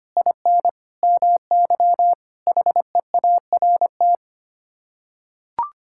Morse code, heard as a keyed tone, about 25 wpm, 700 Hz. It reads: IN MY HEART
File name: Sets of 9 Words - 25wpm